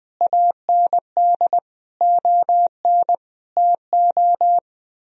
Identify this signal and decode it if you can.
Morse code, heard as a keyed tone, about 21 wpm, 700 Hz. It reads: AND ON TO
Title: Sentences from Top 1000 Words - Review Part 2 - 20wpm